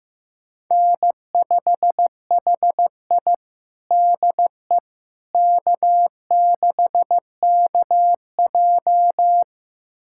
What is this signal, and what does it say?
Morse code, heard as a keyed tone, about 15 wpm, 700 Hz. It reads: N5HI DE K6KJ
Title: USA Callsign DE Callsign - 4 Characters - 15wpm